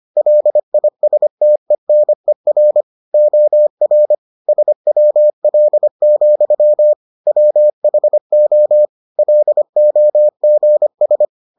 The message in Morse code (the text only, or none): LISTENER OR SWL, WHO LOGS